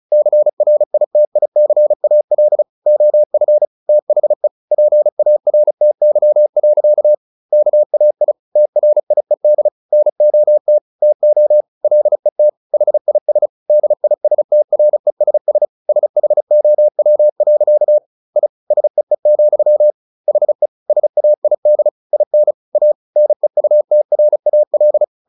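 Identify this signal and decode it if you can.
Morse code, heard as a keyed tone, about 34 wpm, 600 Hz. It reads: CRITICAL OF THE PARTY. KAI TRIED NOT TO LET HIS DISTRESS SHOW. I SEE, HE SAID IN A NEUTRAL
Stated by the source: Morse practice at 35 wpm from CW med Gnister, sending Ken Follett's novel Never in English.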